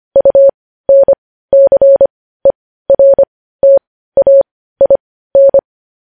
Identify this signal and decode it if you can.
Morse code, heard as a keyed tone, about 25 wpm, spaced as for 15 wpm, 550 Hz. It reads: UNCERTAIN
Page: news headlines in morse code at 15 wpm